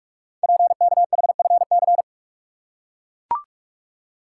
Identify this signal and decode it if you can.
Morse code, heard as a keyed tone, about 44 wpm, 700 Hz. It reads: PKHFC
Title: Code Groups - 5 Letters - 45wpm